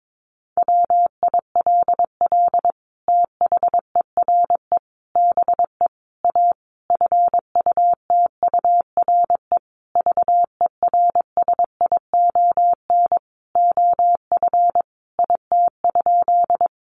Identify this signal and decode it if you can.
Morse code, heard as a keyed tone, about 22 wpm, 700 Hz. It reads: WILL THERE BE A FUTURE VERSION OF IT?
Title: Sentences from Top 1400 Words 22wpm